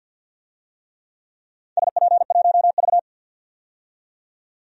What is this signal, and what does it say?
Morse code, heard as a keyed tone, about 47 wpm, 700 Hz. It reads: SP1V
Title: DXCC-100 Call Signs - 4 Characters - 50wpm